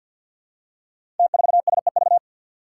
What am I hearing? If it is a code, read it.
Morse code, heard as a keyed tone, about 51 wpm, 700 Hz. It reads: T4SEV
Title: Five Challenging Letter-Number Combinations 50wpm